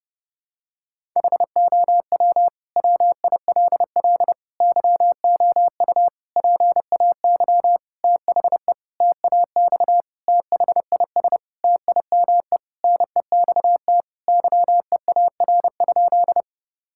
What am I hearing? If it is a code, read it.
Morse code, heard as a keyed tone, about 30 wpm, 700 Hz. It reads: HOW WILL YOU PAY THE TAX THIS TIME NEXT YEAR?
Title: Sentences from Top 1000 Words - Review Part 5 - 30wpm